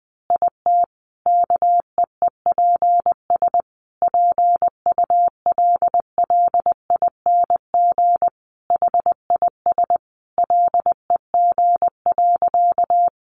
Morse code, heard as a keyed tone, about 19 wpm, 700 Hz. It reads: IT KEEPS PULLING HIS LEG.
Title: Additional Easy Phrases 20wpm